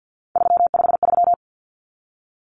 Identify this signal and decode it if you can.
Morse code, heard as a keyed tone, about 49 wpm, 700 Hz. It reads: 353
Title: Numbers - 3-Digit 50wpm